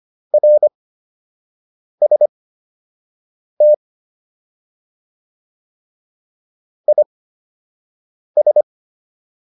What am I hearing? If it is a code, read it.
Morse code, heard as a keyed tone, about 25 wpm, spaced as for 6 wpm, 600 Hz. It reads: RST IS